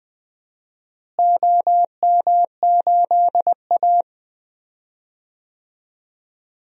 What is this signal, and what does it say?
Morse code, heard as a keyed tone, about 20 wpm, 700 Hz. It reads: OM8A
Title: DXCC-100 Call Signs - 4 Characters - Repeated 2x - 20wpm